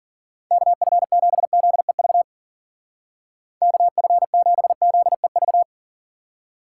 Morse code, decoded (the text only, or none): KF7ZEV KF7ZEV